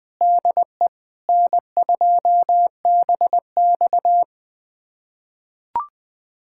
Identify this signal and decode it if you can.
Morse code, heard as a keyed tone, about 20 wpm, 700 Hz. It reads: DE N2BX
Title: USA Callsign DE Callsign - 4 Characters - 20wpm